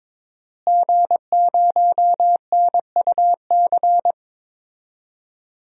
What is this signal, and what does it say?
Morse code, heard as a keyed tone, about 22 wpm, 700 Hz. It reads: G0NUC
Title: DXCC-100 Call Signs - 5 Characters - Repeated 2x - 22wpm